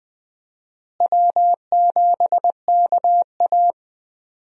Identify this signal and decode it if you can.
Morse code, heard as a keyed tone, about 20 wpm, 700 Hz. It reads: W7KA
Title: USA Call Signs - 4 Characters - Repeated 2x - 20wpm